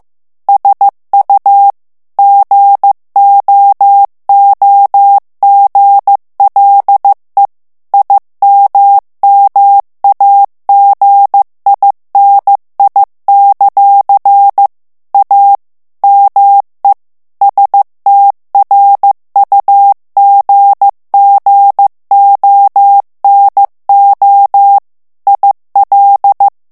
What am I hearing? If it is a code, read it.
Morse code, heard as a keyed tone, about 15 wpm, 800 Hz. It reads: SU GOOGLE IMMAGINI; A ME STRUGGONO IL
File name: ITALIEN texte clair 1 vitesse 14 mots minute